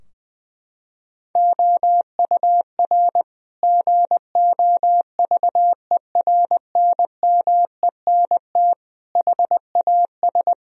Morse code, heard as a keyed tone, about 20 wpm, 700 Hz. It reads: OUR GOVERNMENT HAS
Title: Sentences from Top 1000 Words - Review Part 4 - 20wpm